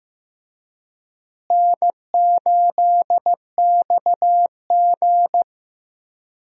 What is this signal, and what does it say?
Morse code, heard as a keyed tone, about 15 wpm, 700 Hz. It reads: N8XG
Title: USA Call Signs - 4 Characters - Repeated 3x - 15wpm